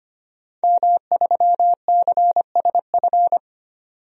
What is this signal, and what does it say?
Morse code, heard as a keyed tone, about 25 wpm, 700 Hz. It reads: M3CSF
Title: DXCC-100 Call Signs - 5 Characters - 25wpm